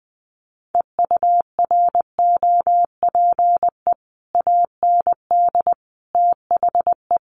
Morse code, heard as a keyed tone, about 20 wpm, 700 Hz. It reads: EUROPE AND THE
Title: Sets of 5 Words Encore2 20wpm